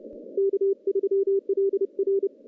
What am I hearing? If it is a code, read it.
Morse code, heard as a keyed tone, about 30 wpm, 390 Hz. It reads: K3LR
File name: contests / CW-SS-k5tr- -1400